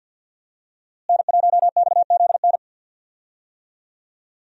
Morse code, heard as a keyed tone, about 50 wpm, 700 Hz. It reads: N1XZN
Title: USA Call Signs - 5 Characters - Repeated 2x - 50wpm